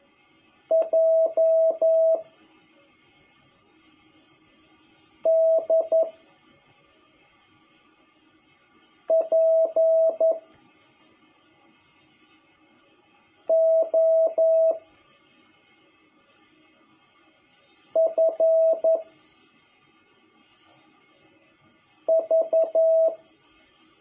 Morse code, heard as a keyed tone, about 11 wpm, 650 Hz. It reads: J D P O F V
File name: jh8cbh / 2021cw